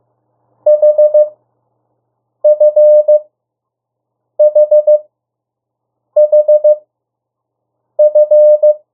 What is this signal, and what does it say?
Morse code, heard as a keyed tone, about 15 wpm, spaced as for 6 wpm, 600 Hz. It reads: HFHHF
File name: morse mp3 / 02 urok 1